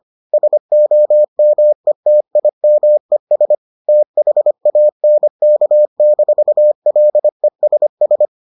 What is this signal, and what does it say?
Morse code, heard as a keyed tone, about 25 wpm, 600 Hz. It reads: SOMETIMES THANK-LESS